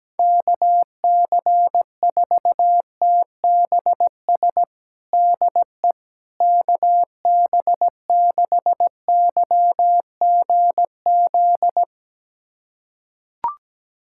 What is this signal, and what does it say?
Morse code, heard as a keyed tone, about 17 wpm, 700 Hz. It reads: KC4TBS DE KB6YGZ
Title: USA Callsign DE Callsign - 6 Characters - 17wpm